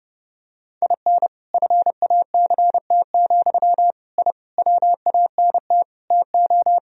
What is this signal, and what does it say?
Morse code, heard as a keyed tone, about 30 wpm, 700 Hz. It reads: IN FACT, I WANT TO
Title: Sentences from Top 1000 Words - Review Part 3 - 30wpm